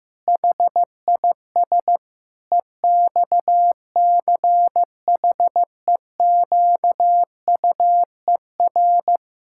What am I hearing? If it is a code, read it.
Morse code, heard as a keyed tone, about 15 wpm, 700 Hz. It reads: HIS EXCHEQUER
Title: Pangram Challenge v2 - 15wpm